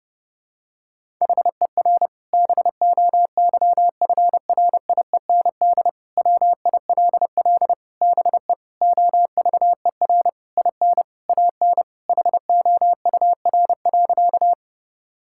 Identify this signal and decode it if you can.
Morse code, heard as a keyed tone, about 30 wpm, 700 Hz. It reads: HER BOYFRIEND WILL BE OVER IN AN HOUR.
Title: Sentences from Top 1000 Words - Review Part 5 - 30wpm